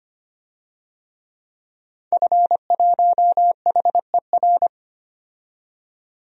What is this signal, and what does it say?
Morse code, heard as a keyed tone, about 25 wpm, 700 Hz. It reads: F1HER
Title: DXCC-100 Call Signs - 5 Characters - 25wpm